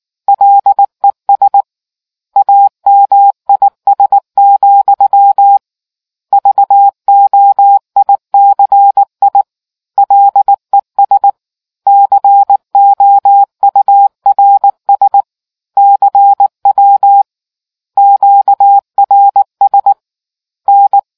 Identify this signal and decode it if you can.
Morse code, heard as a keyed tone, about 18 wpm, 800 Hz. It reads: LES AMIS, VOICI LES COURS CW QRS N